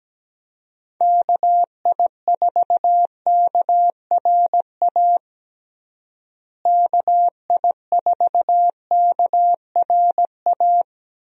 Morse code, heard as a keyed tone, about 17 wpm, 700 Hz. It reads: KI4KRA KI4KRA
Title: USA Call Signs - 6 Characters - Repeated 3x - 17wpm